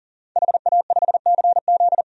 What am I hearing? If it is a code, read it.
Morse code, heard as a keyed tone, about 40 wpm, 700 Hz. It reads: HA5CZ
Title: DXCC-100 Call Signs - 5 Characters - 40wpm